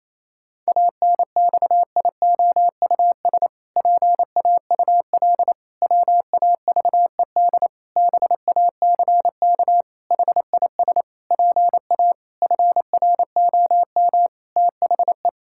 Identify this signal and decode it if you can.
Morse code, heard as a keyed tone, about 28 wpm, 700 Hz. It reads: ANXIOUS PAUL WAVED BACK HIS PA FROM THE
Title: Pangram Challenge v2 - 28wpm